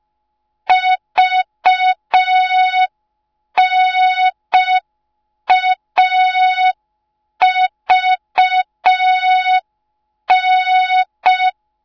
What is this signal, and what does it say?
Morse code, heard as a keyed tone, about 5 wpm, 3700 Hz. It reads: VNAVN